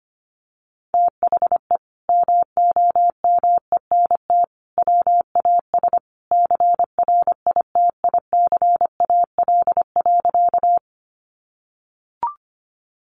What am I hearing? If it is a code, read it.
Morse code, heard as a keyed tone, about 25 wpm, 700 Hz. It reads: THE MOMENT WAS CRITICAL.
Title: Sentences from Top 1100 Words 25wpm